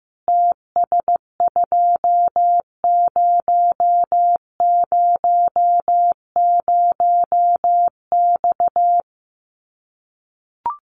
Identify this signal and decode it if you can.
Morse code, heard as a keyed tone, about 15 wpm, 700 Hz. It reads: TS2000X
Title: Common QSO Phrases 15wpm